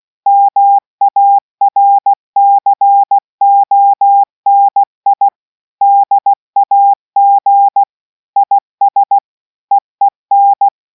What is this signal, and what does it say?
Morse code, heard as a keyed tone, about 16 wpm, 800 Hz. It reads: MARCONI DAG IS EEN